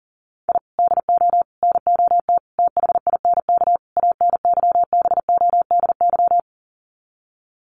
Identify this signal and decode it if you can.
Morse code, heard as a keyed tone, about 40 wpm, 700 Hz. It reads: I DO NOT THINK ANYBODY
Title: Sets of 5 Words 40wpm